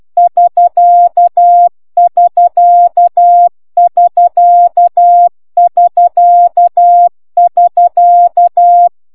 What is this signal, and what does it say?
Morse code, heard as a keyed tone, about 12 wpm, 1700 Hz, with noise. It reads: IK IK IK IK IK